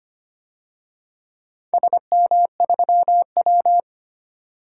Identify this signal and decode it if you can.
Morse code, heard as a keyed tone, about 25 wpm, 700 Hz. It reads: SM3W
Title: DXCC-100 Call Signs - 4 Characters - 25wpm